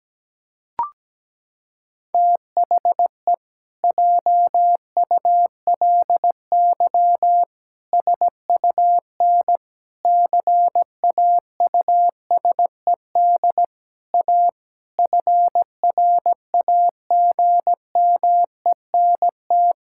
Morse code, heard as a keyed tone, about 17 wpm, 700 Hz. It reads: THE JULY SUN CAUSED A FRAGMENT
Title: Pangram Challenge v2 - 17wpm